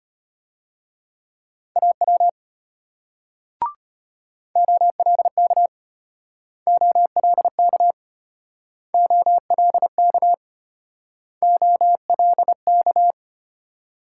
Morse code, heard as a keyed tone, about 34 wpm, 700 Hz. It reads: AW OLK OLK OLK OLK